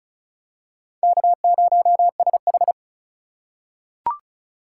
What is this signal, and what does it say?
Morse code, heard as a keyed tone, about 34 wpm, 700 Hz. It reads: K0SH
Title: USA Call Signs - 4 Characters - 3x Character Spacing - 35wpm